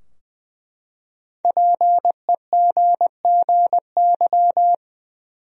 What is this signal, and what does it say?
Morse code, heard as a keyed tone, about 20 wpm, 700 Hz. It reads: PEGGY
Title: USA Common Names - 20wpm